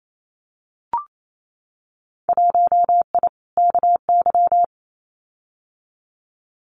Morse code, heard as a keyed tone, about 28 wpm, 700 Hz. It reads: 1I KY
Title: Winter Field Day Exchanges 28wpm